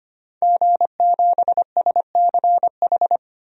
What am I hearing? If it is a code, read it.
Morse code, heard as a keyed tone, about 25 wpm, 700 Hz. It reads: G7SCH
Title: DXCC-100 Call Signs - 5 Characters - 25wpm